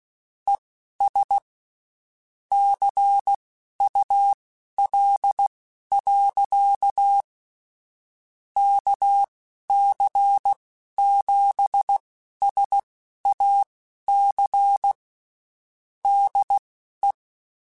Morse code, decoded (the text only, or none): ES CUL. KC7SAC DE